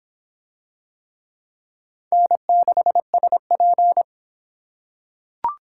Morse code, heard as a keyed tone, about 26 wpm, 700 Hz. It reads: N6SP